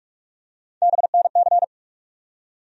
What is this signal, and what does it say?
Morse code, heard as a keyed tone, about 47 wpm, 700 Hz. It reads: BNC
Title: USA Call Sign Postfixes - 3 Letters 45wpm